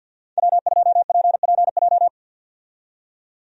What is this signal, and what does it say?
Morse code, heard as a keyed tone, about 50 wpm, 700 Hz. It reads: W2PPJ
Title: USA Call Signs - 5 Characters - 50wpm